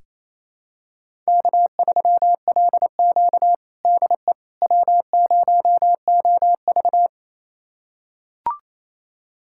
Morse code, heard as a keyed tone, about 28 wpm, 700 Hz. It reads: K3LQ DE W0OV
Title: USA Callsign DE Callsign - 4 Characters - 28wpm